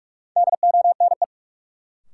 Morse code, heard as a keyed tone, about 46 wpm, 700 Hz. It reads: DONE
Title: Sets of 5 Words Encore - New Word Drill - 45wpm